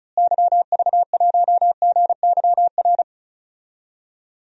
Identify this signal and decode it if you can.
Morse code, heard as a keyed tone, about 36 wpm, 700 Hz. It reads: YV1GYR
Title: DXCC-100 Call Signs - 6 Characters - Repeated 2x - 35wpm